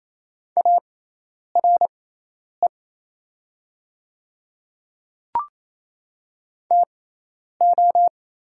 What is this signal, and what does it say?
Morse code, heard as a keyed tone, about 27 wpm, spaced as for 10 wpm, 700 Hz. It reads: ARE TO
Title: 6x Character Spacing - Top 100 Words - 28wpm